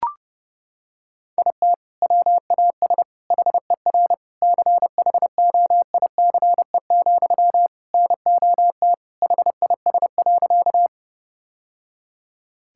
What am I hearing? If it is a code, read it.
Morse code, heard as a keyed tone, about 32 wpm, 700 Hz. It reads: IT WAS HER CHOICE, NOT HIS.
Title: Sentences from Top 1000 Words - Review Part 6 - 30wpm